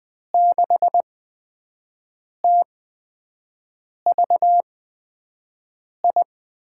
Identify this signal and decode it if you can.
Morse code, heard as a keyed tone, about 20 wpm, spaced as for 5 wpm, 700 Hz. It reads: 6TVI
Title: USA Call Signs - 6 Characters - 8x Character Spacing - 20wpm